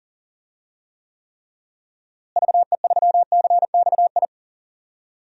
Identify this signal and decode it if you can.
Morse code, heard as a keyed tone, about 40 wpm, 700 Hz. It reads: VE3CXI
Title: DXCC-100 Call Signs - 6 Characters - Repeated 3x - 40wpm